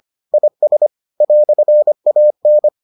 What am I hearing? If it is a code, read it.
Morse code, heard as a keyed tone, about 25 wpm, 600 Hz. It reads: IS "AN